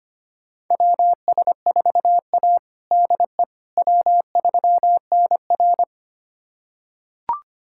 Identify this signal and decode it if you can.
Morse code, heard as a keyed tone, about 25 wpm, 700 Hz. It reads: WS4A DE W3NR
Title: USA Callsign DE Callsign - 4 Characters - 25wpm